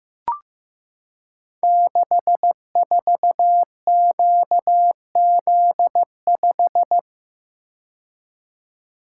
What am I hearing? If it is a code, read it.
Morse code, heard as a keyed tone, about 15 wpm, 700 Hz. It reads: 64QZ5
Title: Code Groups - 5 Letter:Numbers - 15wpm